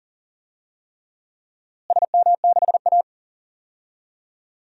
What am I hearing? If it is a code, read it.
Morse code, heard as a keyed tone, about 40 wpm, 700 Hz. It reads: SM6A
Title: DXCC-100 Call Signs - 4 Characters - Repeated 3x - 40wpm